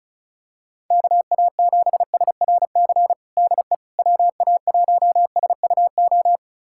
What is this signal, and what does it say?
Morse code, heard as a keyed tone, about 34 wpm, 700 Hz. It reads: KA7SRC DE WA1SUO
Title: USA Callsign DE Callsign - 6 Characters - 35wpm